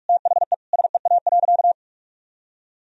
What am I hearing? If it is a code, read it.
Morse code, heard as a keyed tone, about 46 wpm, 700 Hz. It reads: THE SEA.
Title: Sentences from Top 1000 Words - Review Part 1 - 45wpm